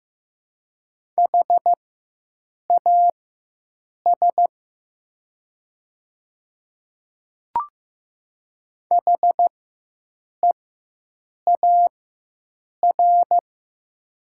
Morse code, heard as a keyed tone, about 15 wpm, spaced as for 7 wpm, 700 Hz. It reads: HAS HEAR
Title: 4x Character Spacing - Top 300 Words - 15wpm